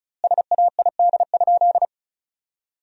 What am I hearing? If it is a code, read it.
Morse code, heard as a keyed tone, about 36 wpm, 700 Hz. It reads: SAID?
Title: 2x Word-Spacing - Sentences from Top 100 Words - 35wpm